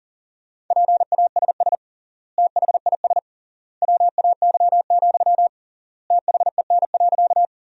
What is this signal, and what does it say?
Morse code, heard as a keyed tone, about 40 wpm, 700 Hz. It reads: PASS THIS WAY, THEN.
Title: 3x Word-Spacing - Sentences from Top 400 Words - 40wpm